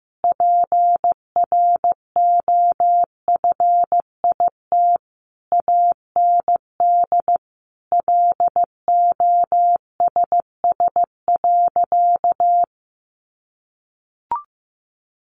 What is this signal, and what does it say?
Morse code, heard as a keyed tone, about 15 wpm, 700 Hz. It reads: PROFIT AND LOSS.
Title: Sentences from Top 1400 Words 15wpm